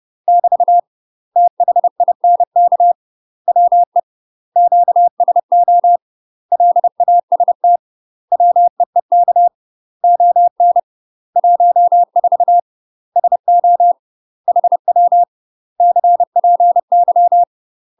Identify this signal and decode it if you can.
Morse code, heard as a keyed tone, about 30 wpm, 700 Hz. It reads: <BT> THINK WE QSO LAST WEEK ON 14 SO HW CPY
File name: UK MASTER QSOs TRAINING/30WPM